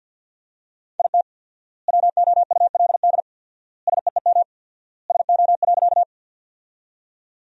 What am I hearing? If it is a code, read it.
Morse code, heard as a keyed tone, about 50 wpm, 700 Hz. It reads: IT WOULD SEEM SO.
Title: 4x Word-Spacing - Sentences from Top 300 Words - 50wpm